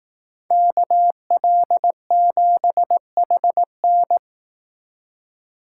Morse code, heard as a keyed tone, about 18 wpm, 700 Hz. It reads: KL7HN
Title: Speed Racing - USA 2x2 Call Signs 15wpm